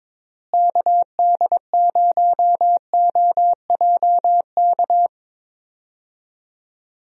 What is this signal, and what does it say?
Morse code, heard as a keyed tone, about 22 wpm, 700 Hz. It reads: KD0OJK
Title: USA Call Signs - 6 Characters - Repeated 3x - 22wpm